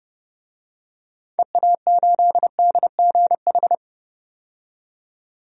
Morse code, heard as a keyed tone, about 30 wpm, 700 Hz. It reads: EA8DGH
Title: DXCC-100 Call Signs - 6 Characters - 30wpm